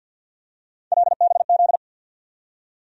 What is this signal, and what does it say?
Morse code, heard as a keyed tone, about 53 wpm, 700 Hz. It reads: LBZ
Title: USA Call Sign Postfixes - 3 Letters 50wpm